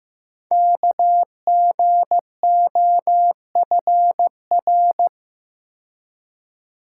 Morse code, heard as a keyed tone, about 15 wpm, 700 Hz. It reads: KGOFR
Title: Code Groups - 5 Letters - 15wpm